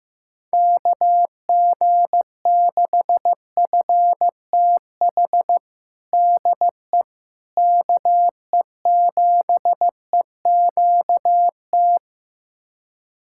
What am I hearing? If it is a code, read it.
Morse code, heard as a keyed tone, about 15 wpm, 700 Hz. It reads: KG6FTH DE KE7EQT
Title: USA Callsign DE Callsign - 6 Characters - 15wpm